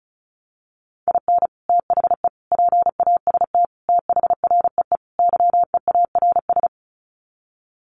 Morse code, heard as a keyed tone, about 34 wpm, 700 Hz. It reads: IN THE PAST THREE YEARS
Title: Sets of 5 Words Encore2 35wpm